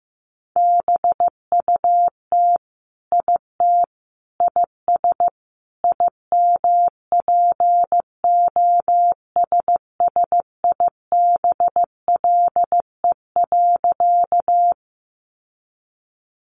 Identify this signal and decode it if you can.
Morse code, heard as a keyed tone, about 15 wpm, 700 Hz. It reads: BUT IT IS IMPOSSIBLE.
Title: Sentences from Top 1100 Words 15wpm